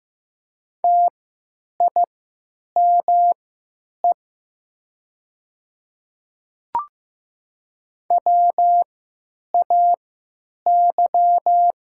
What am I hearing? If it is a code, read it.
Morse code, heard as a keyed tone, about 16 wpm, spaced as for 9 wpm, 700 Hz. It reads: TIME WAY
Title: 3x Character Spacing - Top 100 Words - 15wpm